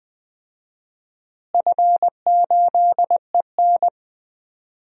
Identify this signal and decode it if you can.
Morse code, heard as a keyed tone, about 20 wpm, 700 Hz. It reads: F8EN
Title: DXCC-100 Call Signs - 4 Characters - Repeated 2x - 20wpm